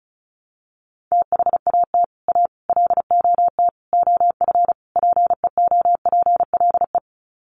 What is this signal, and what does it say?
Morse code, heard as a keyed tone, about 34 wpm, 700 Hz. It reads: THAT A LOT OF PEOPLE
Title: Sets of 5 Words 35wpm